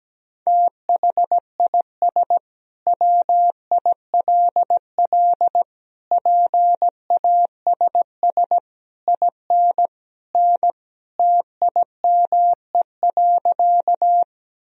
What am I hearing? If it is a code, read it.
Morse code, heard as a keyed tone, about 17 wpm, 700 Hz. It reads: THIS WILL PASS IN N TIME.
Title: Sentences from Top 400 Words and Abbreviations 17wpm